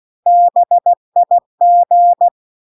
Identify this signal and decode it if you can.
Morse code, heard as a keyed tone, about 16 wpm, 700 Hz. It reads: BIG
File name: CW-The-Big-Twoheaded-River-1-16WPM